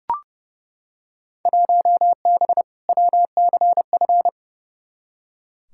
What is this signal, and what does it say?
Morse code, heard as a keyed tone, about 32 wpm, 700 Hz. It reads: 1B WCF
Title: Field Day Exchanges Encore - 30wpm